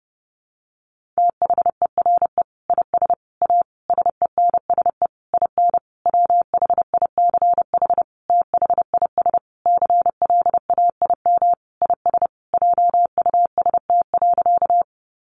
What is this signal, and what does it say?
Morse code, heard as a keyed tone, about 30 wpm, 700 Hz. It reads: THERE IS A SENSE IN WHICH THIS CLAIM IS JUST.
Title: Sentences from Top 1500 Words 30wpm